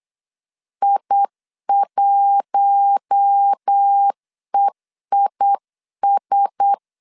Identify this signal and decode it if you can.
Morse code, heard as a keyed tone, about 8 wpm, 800 Hz. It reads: I1EIS
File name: CW / DET / 1 6 1500C S B 8wpm 5142